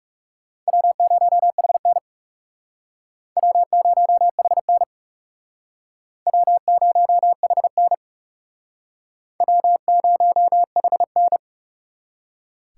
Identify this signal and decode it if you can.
Morse code, heard as a keyed tone, about 36 wpm, 700 Hz. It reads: W0HN W0HN W0HN W0HN
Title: Speed Racing - USA 1x2 Call Signs 30wpm